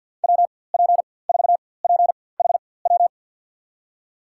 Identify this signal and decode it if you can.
Morse code, heard as a keyed tone, about 53 wpm, spaced as for 24 wpm, 700 Hz. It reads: WP4PHW
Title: USA Call Signs - 6 Characters - 4x Character Spacing - 50wpm